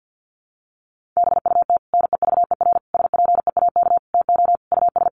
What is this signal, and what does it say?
Morse code, heard as a keyed tone, about 50 wpm, 700 Hz. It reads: BUT NEVER SPEAK TO US
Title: Sentences from Top 1100 Words 50wpm